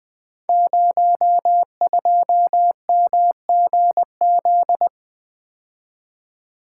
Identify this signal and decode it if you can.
Morse code, heard as a keyed tone, about 20 wpm, 700 Hz. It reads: 02MGZ
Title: Code Groups - 5 Letter:Numbers - 20wpm